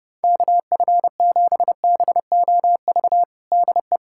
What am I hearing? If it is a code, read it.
Morse code, heard as a keyed tone, about 32 wpm, 700 Hz. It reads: KF7BOV DE
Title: USA Callsign DE Callsign 30wpm